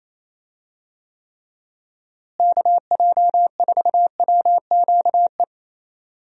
Morse code, heard as a keyed tone, about 28 wpm, 700 Hz. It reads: KJ4WQE
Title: USA Call Signs - 6 Characters - Repeated 2x - 28wpm